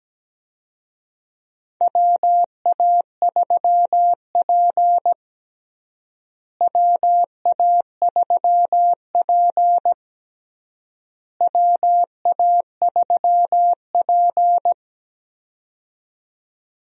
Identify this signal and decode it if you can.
Morse code, heard as a keyed tone, about 17 wpm, 700 Hz. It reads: WA3P WA3P WA3P
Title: USA Call Signs - 4 Characters - Repeated 3x - 17wpm